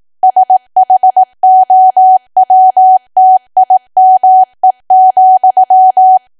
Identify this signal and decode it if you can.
Morse code, heard as a keyed tone, about 18 wpm, 750 Hz. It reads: SHOWTIME,
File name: Archive / 18 WPM / 18WPM